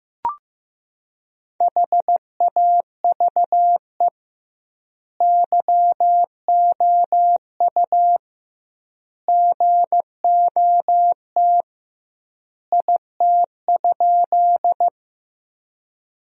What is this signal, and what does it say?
Morse code, heard as a keyed tone, about 15 wpm, 700 Hz. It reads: HAVE YOU GOT IT?
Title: 2x Word-Spacing - Sentences from Top 300 Words - 15wpm